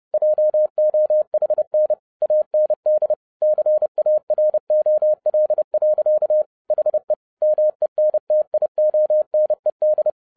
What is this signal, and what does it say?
Morse code, heard as a keyed tone, about 30 wpm, 610 Hz. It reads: JOHN AND CAROL. HE MENTIONED